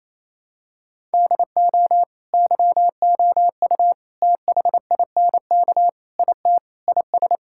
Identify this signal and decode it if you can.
Morse code, heard as a keyed tone, about 28 wpm, 700 Hz. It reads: DO YOU THINK IT IS
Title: Sentences from Top 1000 Words - Review Part 6 - 28wpm